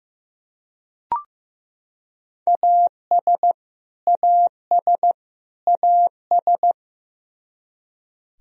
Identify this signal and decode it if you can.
Morse code, heard as a keyed tone, about 15 wpm, 700 Hz. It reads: AS AS AS
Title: Top 100 Words Repeated 15wpm